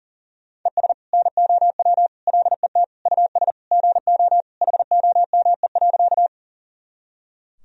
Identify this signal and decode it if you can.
Morse code, heard as a keyed tone, about 37 wpm, 700 Hz. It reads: ES NOW LET US GO HOME.
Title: Sentences from Top 300 Words and Abbreviations 40wpm